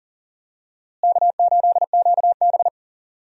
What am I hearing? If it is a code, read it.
Morse code, heard as a keyed tone, about 40 wpm, 700 Hz. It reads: K8QB
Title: USA Call Signs - 4 Characters - Repeated 3x - 40wpm